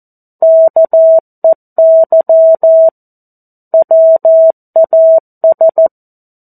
Morse code, heard as a keyed tone, about 14 wpm, 650 Hz. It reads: KEY WAS
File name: 02-Chapter-02 14 WPM